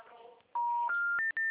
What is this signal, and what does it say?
Special information tone.
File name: out- -2041- - - .58333